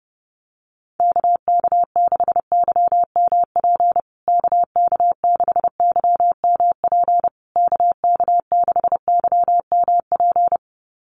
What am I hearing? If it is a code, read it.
Morse code, heard as a keyed tone, about 30 wpm, 700 Hz. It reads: KK6YMP KK6YMP KK6YMP
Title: USA Call Signs Repeated 3x 30wpm